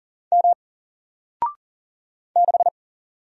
Morse code, heard as a keyed tone, about 40 wpm, 700 Hz. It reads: M 6
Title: Single Letter-Number Alt 40wpm